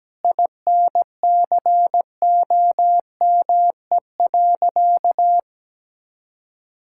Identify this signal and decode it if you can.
Morse code, heard as a keyed tone, about 17 wpm, 700 Hz. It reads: INCOME.
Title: Sentences from Top 1000 Words - Review Part 5 - 17wpm